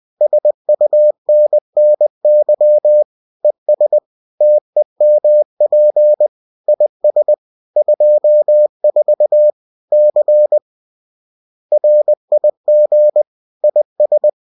Morse code, heard as a keyed tone, about 20 wpm, 600 Hz. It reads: SUNNY ES TEMP IS 24 C RIG IS